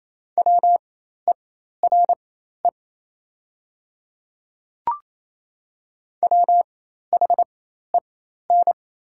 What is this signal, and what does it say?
Morse code, heard as a keyed tone, about 28 wpm, spaced as for 13 wpm, 700 Hz. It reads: WERE WHEN
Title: 4x Character Spacing - Top 500 Words - 28wpm